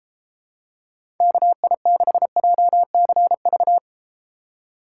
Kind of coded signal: Morse code

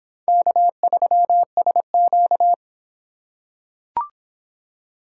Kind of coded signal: Morse code

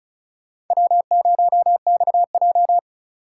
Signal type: Morse code

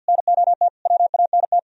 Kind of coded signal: Morse code

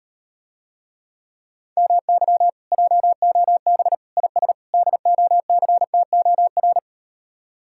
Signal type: Morse code

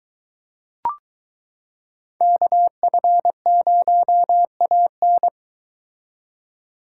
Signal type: Morse code

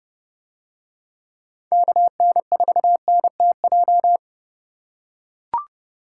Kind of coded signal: Morse code

